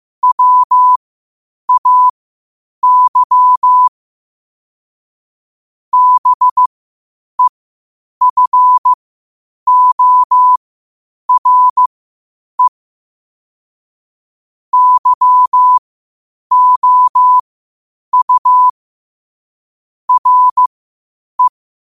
Morse code, heard as a keyed tone, about 15 wpm, spaced as for 8 wpm, 1000 Hz. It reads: WAY BEFORE YOU RE
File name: qotdfiles / 8wpm